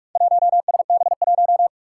Morse code, heard as a keyed tone, about 42 wpm, 700 Hz. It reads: 1SB1
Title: Four Challenging Letter-Number Combinations 45wpm